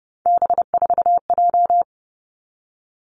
Morse code, heard as a keyed tone, about 30 wpm, 700 Hz. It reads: B4J
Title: Three Challenging Letter-Number Combinations 30wpm